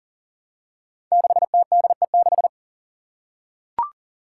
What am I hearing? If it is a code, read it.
Morse code, heard as a keyed tone, about 40 wpm, 700 Hz. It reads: 6TDE6
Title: Five Challenging Letter-Number Combinations 40wpm